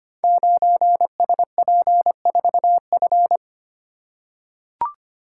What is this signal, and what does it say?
Morse code, heard as a keyed tone, about 25 wpm, 700 Hz. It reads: 9SP4F
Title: Code Groups - 5 Letter:Numbers - 25wpm